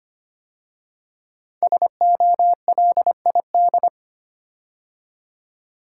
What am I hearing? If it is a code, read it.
Morse code, heard as a keyed tone, about 25 wpm, 700 Hz. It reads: SOLID
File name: 5 Letter Words - First 3 Letters Given - 25wpm